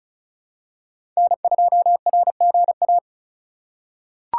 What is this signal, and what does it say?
Morse code, heard as a keyed tone, about 36 wpm, 700 Hz. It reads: N2RGA
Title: USA Call Signs - 5 Characters - 3x Character Spacing - 35wpm